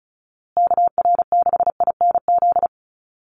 Morse code, heard as a keyed tone, about 34 wpm, 700 Hz. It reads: KR6INZ
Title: USA Call Signs Repeated 2x 35wpm